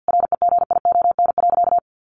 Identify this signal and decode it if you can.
Morse code, heard as a keyed tone, about 50 wpm, 700 Hz. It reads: REGION.
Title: Sentences from Top 1500 Words 50wpm